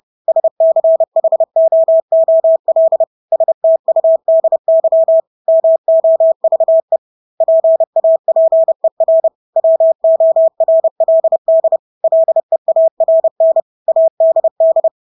Morse code, heard as a keyed tone, about 30 wpm, 650 Hz. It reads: SCHOOL STUDY MOVE PAPER WORLD LEARN ADD
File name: Engelske ord 30-30 wpm